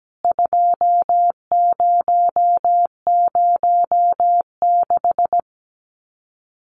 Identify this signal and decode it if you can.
Morse code, heard as a keyed tone, about 17 wpm, 700 Hz. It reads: 2006
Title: CWT Exchange 17wpm